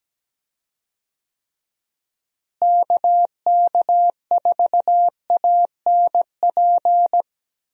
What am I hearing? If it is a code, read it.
Morse code, heard as a keyed tone, about 17 wpm, 700 Hz. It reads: KK4ANP
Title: USA Call Signs - 6 Characters - Repeated 2x - 17wpm